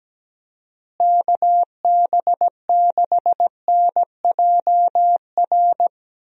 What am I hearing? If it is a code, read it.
Morse code, heard as a keyed tone, about 17 wpm, 700 Hz. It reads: KB6NJR
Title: USA Call Signs - 6 Characters - Repeated 3x - 17wpm